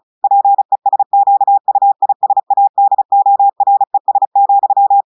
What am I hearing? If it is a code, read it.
Morse code, heard as a keyed tone, about 35 wpm, 800 Hz. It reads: PESQUISADORES,
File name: audios / portugais / portugais texte en clair 1 vitesse 35 mots minutes